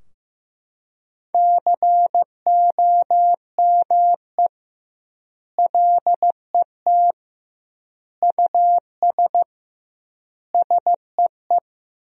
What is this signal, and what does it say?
Morse code, heard as a keyed tone, about 16 wpm, 700 Hz. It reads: COME LET US SEE
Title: 2x Word-Spacing - Sentences from Top 300 Words - 15wpm